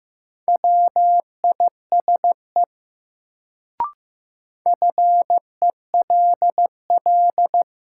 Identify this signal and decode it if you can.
Morse code, heard as a keyed tone, about 15 wpm, 700 Hz. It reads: WISE FELL
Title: Sentences from Top 1600 Words - New Word Drill - 15wpm